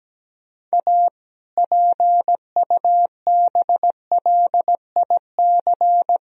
Morse code, heard as a keyed tone, about 17 wpm, 700 Hz. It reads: A PUBLIC
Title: Sentences from Top 1000 Words - Review Part 3 - 17wpm